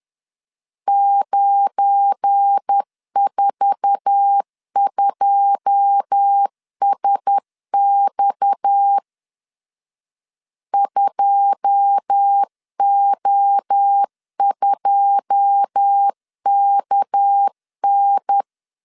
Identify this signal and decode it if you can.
Morse code, heard as a keyed tone, about 10 wpm, 800 Hz. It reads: 942SX 2O2KN
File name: LEZIONI / 9 2 10wpm 2736